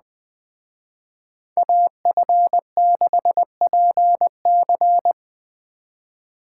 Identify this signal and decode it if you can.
Morse code, heard as a keyed tone, about 20 wpm, 700 Hz. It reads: AF6PC